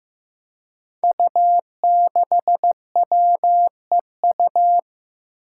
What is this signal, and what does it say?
Morse code, heard as a keyed tone, about 15 wpm, 700 Hz. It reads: U6WEU
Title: Five Challenging Letter-Number Combinations 15wpm